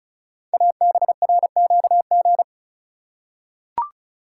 Morse code, heard as a keyed tone, about 36 wpm, 690 Hz. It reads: ABRQG
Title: Code Groups - 5 Letters - 35wpm